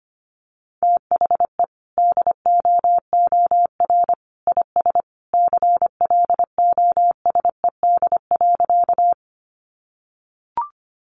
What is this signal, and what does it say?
Morse code, heard as a keyed tone, about 25 wpm, 700 Hz. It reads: THE DOOR IS CLOSED.
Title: Sentences from Top 1400 Words 25wpm